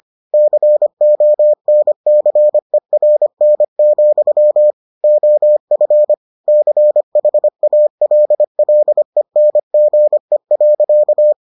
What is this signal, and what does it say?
Morse code, heard as a keyed tone, about 25 wpm, 600 Hz. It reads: CONCERN, OF CHALLENGE.